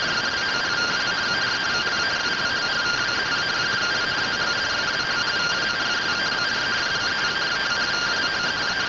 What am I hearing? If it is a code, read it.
Two-tone teleprinter signal, about 170 Hz shift.